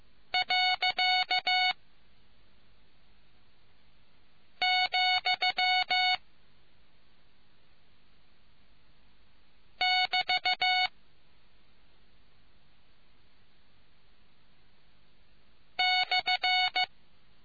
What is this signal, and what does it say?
Morse code, heard as a keyed tone, about 15 wpm, 740 Hz. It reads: . , <BT> /